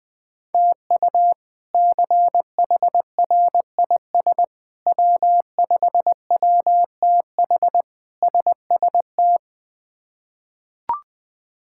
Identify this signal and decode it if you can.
Morse code, heard as a keyed tone, about 21 wpm, 700 Hz. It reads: TU CHRIS W5WTH SST
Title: K1USN SST QSO Practice - 20wpm